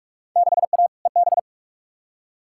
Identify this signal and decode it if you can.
Morse code, heard as a keyed tone, about 42 wpm, 700 Hz. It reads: 6A EB
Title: Field Day Exchanges Encore - 45wpm